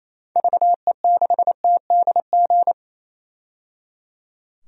Morse code, heard as a keyed tone, about 28 wpm, 700 Hz. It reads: VE6TDG
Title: DXCC-100 Call Signs - 6 Characters - 28wpm